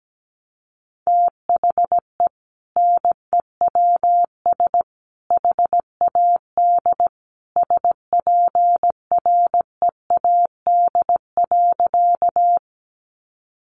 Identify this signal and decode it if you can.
Morse code, heard as a keyed tone, about 17 wpm, 700 Hz. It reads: THE NEWS HAD SPREAD.
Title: Sentences from Top 1200 Words 17wpm